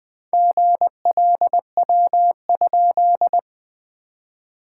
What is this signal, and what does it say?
Morse code, heard as a keyed tone, about 20 wpm, 690 Hz. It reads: GLW?
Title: SOTA QSO Practice 20wpm